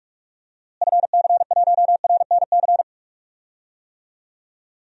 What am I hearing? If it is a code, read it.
Morse code, heard as a keyed tone, about 46 wpm, 700 Hz. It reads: FC1RNC
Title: DXCC-100 Call Signs - 6 Characters - Repeated 3x - 45wpm